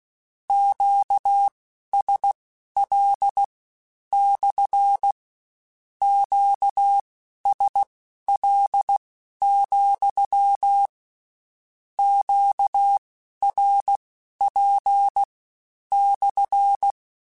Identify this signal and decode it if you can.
Morse code, heard as a keyed tone, about 16 wpm, spaced as for 11 wpm, 800 Hz. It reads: QSL / QSL, QRP /
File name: Q CALLS SPSL @15